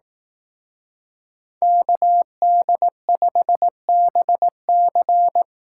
Morse code, heard as a keyed tone, about 18 wpm, 700 Hz. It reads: KD5BC
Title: Speed Racing - USA 2x2 Call Signs 15wpm